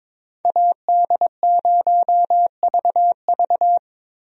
Morse code, heard as a keyed tone, about 22 wpm, 700 Hz. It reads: AD0VV
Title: USA Call Signs - 5 Characters - Repeated 2x - 22wpm